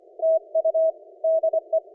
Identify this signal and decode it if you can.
Morse code, heard as a keyed tone, about 25 wpm, 650 Hz. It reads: TU DE
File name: PJ2/PA3EYC - Curacao on 17m CW